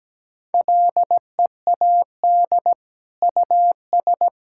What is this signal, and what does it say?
Morse code, heard as a keyed tone, about 17 wpm, 700 Hz. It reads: LEAD US
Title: Sentences from Top 1600 Words 17wpm